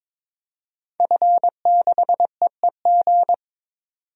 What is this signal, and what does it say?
Morse code, heard as a keyed tone, about 23 wpm, 700 Hz. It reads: F6EEG
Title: DXCC-100 Call Signs - 5 Characters - 22wpm